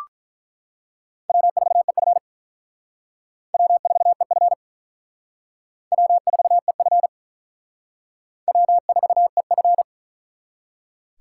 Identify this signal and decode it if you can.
Morse code, heard as a keyed tone, about 42 wpm, 700 Hz. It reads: W4EF W4EF W4EF W4EF
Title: Speed Racing - USA 1x2 Call Signs 35wpm